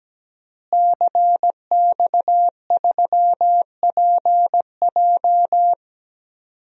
Morse code, heard as a keyed tone, about 17 wpm, 700 Hz. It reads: CX3PJ
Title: DXCC-100 Call Signs - 5 Characters - Repeated 3x - 17wpm